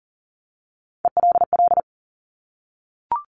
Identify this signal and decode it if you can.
Morse code, heard as a keyed tone, about 41 wpm, 700 Hz. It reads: ELL
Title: Top Three Letter Combinations in Words 40wpm